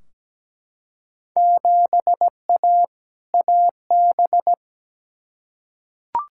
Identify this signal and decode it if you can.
Morse code, heard as a keyed tone, about 17 wpm, 700 Hz. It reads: 7A AB
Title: Field Day Exchanges Encore - 17wpm